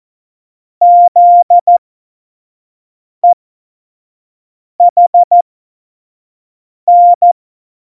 Morse code, heard as a keyed tone, about 14 wpm, spaced as for 5 wpm, 700 Hz. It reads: ZEHN
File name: Morsekurs / Witze / Witz 19 Mai 2020, 25 F70